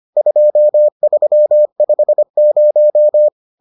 Morse code, heard as a keyed tone, about 25 wpm, 600 Hz. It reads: 2350